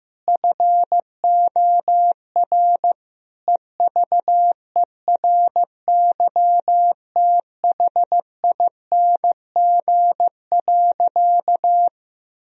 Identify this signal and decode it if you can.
Morse code, heard as a keyed tone, about 16 wpm, 700 Hz. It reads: FOR EVERYTHING.
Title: Sentences from Top 1000 Words - Review Part 3 - 15wpm